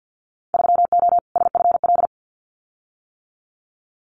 Morse code, heard as a keyed tone, about 50 wpm, 700 Hz. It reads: 3O SFL
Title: Winter Field Day Exchanges 50wpm